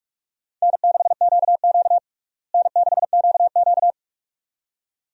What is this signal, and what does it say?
Morse code, heard as a keyed tone, about 43 wpm, 700 Hz. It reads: N6QQ N6QQ
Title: USA Call Signs - 4 Characters - Repeated 2x - 45wpm